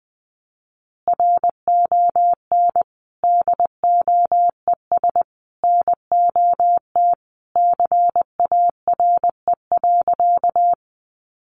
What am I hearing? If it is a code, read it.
Morse code, heard as a keyed tone, about 20 wpm, 700 Hz. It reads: RON DOES NOT CARE.
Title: Additional Easy Phrases 20wpm